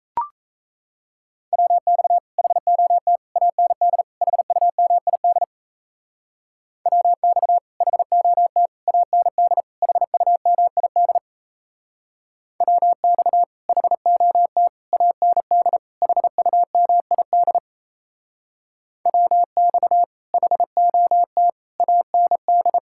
Morse code, heard as a keyed tone, about 34 wpm, 700 Hz. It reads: WX HOT AND HUMID WX HOT AND HUMID WX HOT AND HUMID WX HOT AND
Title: Speed-Racing Common QSO Phrases - 28wpm